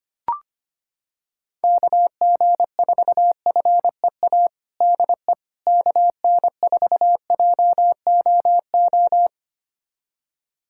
Morse code, heard as a keyed tone, about 25 wpm, 700 Hz. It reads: KG4FEA DE KN4JOO
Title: USA Callsign DE Callsign - 6 Characters - 25wpm